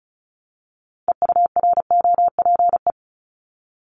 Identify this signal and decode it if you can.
Morse code, heard as a keyed tone, about 34 wpm, 700 Hz. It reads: EUROPE
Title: Multi-letter Phonemes Repeated 3x 35wpm